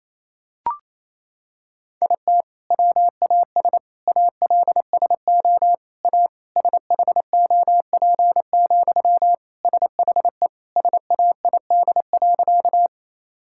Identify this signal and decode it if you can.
Morse code, heard as a keyed tone, about 28 wpm, 700 Hz. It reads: IT WAS ALSO A SHOP, SHE SAID.
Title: Sentences from Top 1600 Words 28wpm